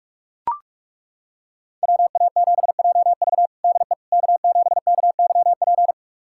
Morse code, heard as a keyed tone, about 41 wpm, 700 Hz. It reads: WA7JV DE K7KYP
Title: USA Callsign DE Callsign - 5 Characters - 45wpm